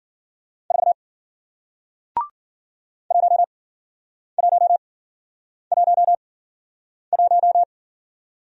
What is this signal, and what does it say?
Morse code, heard as a keyed tone, about 52 wpm, spaced as for 9 wpm, 700 Hz. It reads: 4 1111